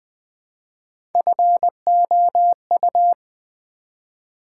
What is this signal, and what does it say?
Morse code, heard as a keyed tone, about 20 wpm, 700 Hz. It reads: FOU
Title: Build A Word - 1 Letter At A Time - 5-Letter Words - 20wpm